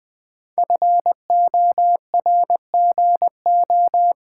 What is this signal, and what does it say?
Morse code, heard as a keyed tone, about 20 wpm, 700 Hz. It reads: FORGO
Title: Sentences from Top 1400 Words - New Word Drill Alt - 20wpm